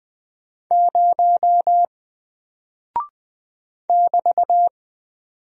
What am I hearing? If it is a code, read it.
Morse code, heard as a keyed tone, about 20 wpm, 700 Hz. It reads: 0 <BT>
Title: Single Character (Letter:Number:Punctuation:Prosign) 20wpm